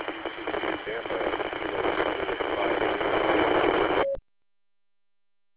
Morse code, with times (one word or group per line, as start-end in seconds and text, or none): none